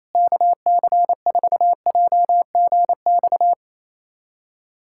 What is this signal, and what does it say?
Morse code, heard as a keyed tone, about 28 wpm, 700 Hz. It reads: KC4JGX
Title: USA Callsign DE Callsign - 6 Characters - 28wpm